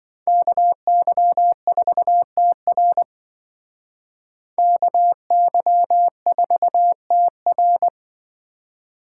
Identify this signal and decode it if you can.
Morse code, heard as a keyed tone, about 25 wpm, 700 Hz. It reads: KY4TR KY4TR
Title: Speed Racing - USA 2x2 Call Signs 20wpm